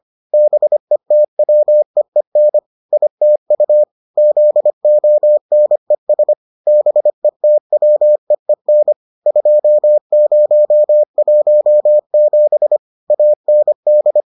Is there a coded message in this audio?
Yes: Morse code from 0.3 to 14.2 s, heard as a keyed tone, about 25 wpm, 600 Hz. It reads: BETWEEN ITU ZONES BETWEEN 2017 AND